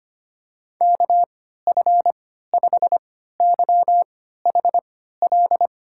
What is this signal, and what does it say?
Morse code, heard as a keyed tone, about 25 wpm, spaced as for 14 wpm, 700 Hz. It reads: KF5YHL
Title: USA Call Signs - 6 Characters - 3x Character Spacing - 25wpm